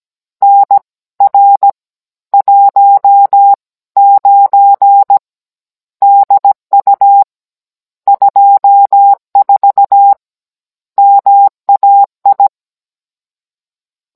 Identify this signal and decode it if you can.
Morse code, heard as a keyed tone, about 17 wpm, spaced as for 14 wpm, 800 Hz. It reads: NR19 DU 24 MAI
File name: Cours-NR19-16mm